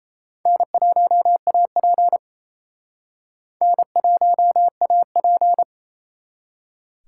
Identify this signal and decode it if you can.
Morse code, heard as a keyed tone, about 34 wpm, 700 Hz. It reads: N1AP N1AP
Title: Speed Racing - USA 1x2 Call Signs 28wpm